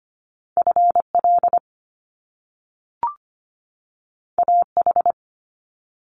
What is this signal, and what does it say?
Morse code, heard as a keyed tone, about 25 wpm, 700 Hz. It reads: F L A H
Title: Top Two Letter Combinations in Words 25wpm